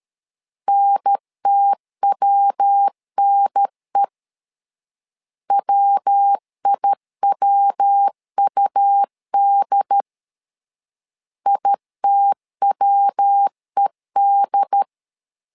Morse code, heard as a keyed tone, about 13 wpm, 800 Hz. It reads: NTWNE WIWUD ITWED
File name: LEZIONI / 3 MWRO / 3 MW 12wpm 1633